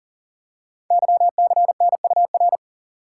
Morse code, heard as a keyed tone, about 39 wpm, 700 Hz. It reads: YCNUR
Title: Code Groups - 5 Letter:Numbers - 40wpm